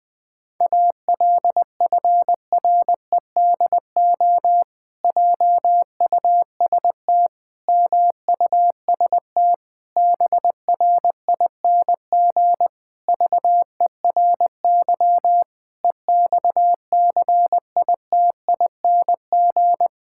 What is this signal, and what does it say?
Morse code, heard as a keyed tone, about 20 wpm, 700 Hz. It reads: ALFREDO JUST MUST BRING VERY EXCITING
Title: Pangram Challenge v2 - 20wpm